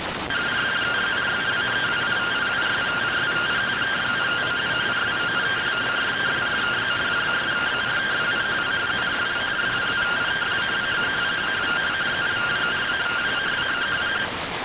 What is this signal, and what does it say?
Two-tone teleprinter signal, about 170 Hz shift.